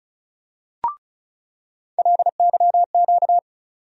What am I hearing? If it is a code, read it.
Morse code, heard as a keyed tone, about 36 wpm, 700 Hz. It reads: LYQ
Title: USA Call Sign Postfixes - 3 Letters 35wpm